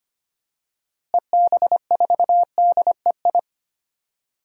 Morse code, heard as a keyed tone, about 25 wpm, 700 Hz. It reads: EB4DEI
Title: DXCC-100 Call Signs - 6 Characters - Repeated 3x - 25wpm